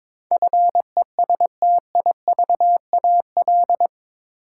Morse code, heard as a keyed tone, about 22 wpm, 700 Hz. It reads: FESTIVAL
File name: practice / 22wpm / Eight Letter Words - 22wpm